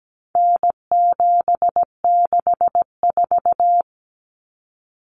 Morse code, heard as a keyed tone, about 17 wpm, 700 Hz. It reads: N764
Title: Common Cut Numbers - 4-Digit 17wpm